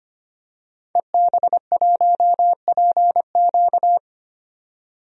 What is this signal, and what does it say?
Morse code, heard as a keyed tone, about 25 wpm, 700 Hz. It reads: EB1PQ
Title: DXCC-100 Call Signs - 5 Characters - 25wpm